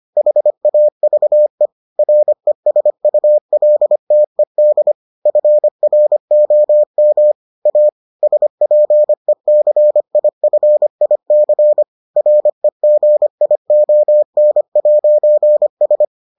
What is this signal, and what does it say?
Morse code, heard as a keyed tone, about 25 wpm, 600 Hz. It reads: HAVE RESULTED FROM A SPECIFIC REGION'S